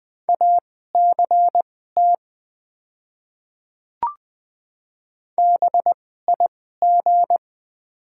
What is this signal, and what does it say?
Morse code, heard as a keyed tone, about 20 wpm, spaced as for 15 wpm, 690 Hz. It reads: ACT BIG
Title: 2x Character Spacing - Top 200 Words - 20wpm